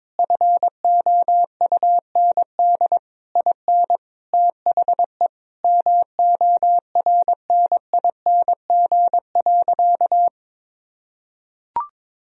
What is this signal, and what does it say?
Morse code, heard as a keyed tone, about 22 wpm, 700 Hz. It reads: FOUND IN THE MORNING.
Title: Sentences from Top 1000 Words - Review Part 1 - 22wpm